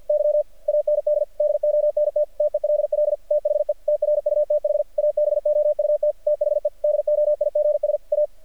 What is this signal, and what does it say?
Morse code, heard as a keyed tone, about 48 wpm, 600 Hz. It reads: Y AND DONT TELL THE TRUTH ABOUT THE DOIGS A
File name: archived / CRSnet